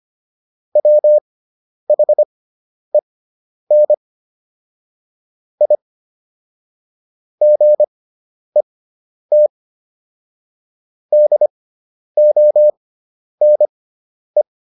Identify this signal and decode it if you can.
Morse code, heard as a keyed tone, about 25 wpm, spaced as for 10 wpm, 600 Hz. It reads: WHEN I GET DONE